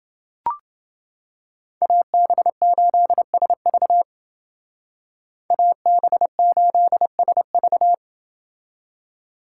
Morse code, heard as a keyed tone, about 28 wpm, 700 Hz. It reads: AB8SV AB8SV